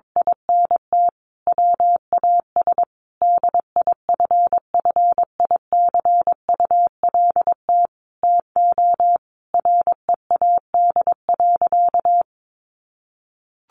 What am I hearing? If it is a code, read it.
Morse code, heard as a keyed tone, about 22 wpm, 700 Hz. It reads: INT WAS DIFFICULT TO READ.
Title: Sentences from Top 1400 Words 22wpm